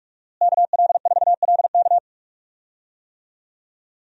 Morse code, heard as a keyed tone, about 42 wpm, 700 Hz. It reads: KL4LK
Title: USA Call Signs - 5 Characters - Repeated 3x - 45wpm